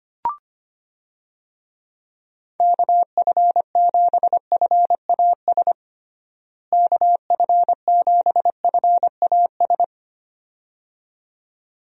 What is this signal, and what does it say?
Morse code, heard as a keyed tone, about 25 wpm, 700 Hz. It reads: KF7FAS KF7FAS
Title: USA Call Signs - 6 Characters - Repeated 2x - 25wpm